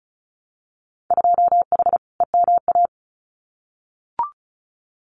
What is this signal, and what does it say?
Morse code, heard as a keyed tone, about 34 wpm, 700 Hz. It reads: 2H EMA
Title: Winter Field Day Exchanges 35wpm